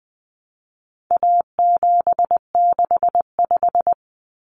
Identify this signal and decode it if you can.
Morse code, heard as a keyed tone, about 20 wpm, 700 Hz. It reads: A765
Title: Common Cut Numbers - 4-Digit 20wpm